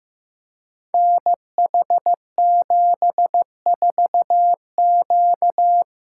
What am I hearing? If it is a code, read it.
Morse code, heard as a keyed tone, about 15 wpm, 700 Hz. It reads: NH74Q
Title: Code Groups - 5 Letter:Numbers - 15wpm